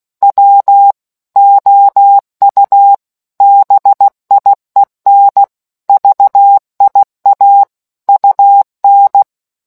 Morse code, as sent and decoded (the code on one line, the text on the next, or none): .-- / --- ..- / -... .. -. / ...- .. .- / ..- -.
W OU BIEN VIA UN